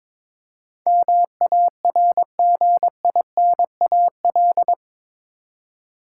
Morse code, heard as a keyed tone, about 22 wpm, 700 Hz. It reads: MARGINAL
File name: practice / 22wpm / Eight Letter Words - 22wpm